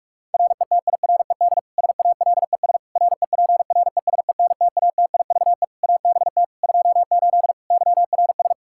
Morse code, heard as a keyed tone, about 42 wpm, 700 Hz. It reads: RETIRED SALES REPRESENTATIVE ABT 28 YRS
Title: Practice QSO - Exchange 3 - More Info - 45wpm